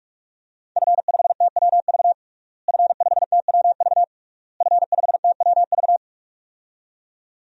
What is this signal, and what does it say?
Morse code, heard as a keyed tone, about 41 wpm, 700 Hz. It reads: F5TWV F5TWV F5TWV
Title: DXCC-100 Call Signs - 5 Characters - Repeated 3x - 45wpm